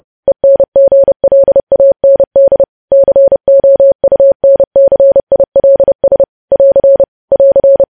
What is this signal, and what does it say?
Morse code, heard as a keyed tone, about 30 wpm, 550 Hz. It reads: ENGLAND COUNCILS <AR> <AR>
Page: news headlines in morse code at 30 wpm